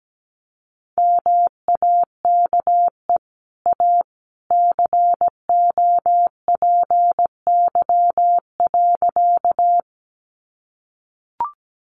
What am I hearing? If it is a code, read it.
Morse code, heard as a keyed tone, about 17 wpm, 700 Hz. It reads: MAKE A COPY.
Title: Sentences from Top 1400 Words 17wpm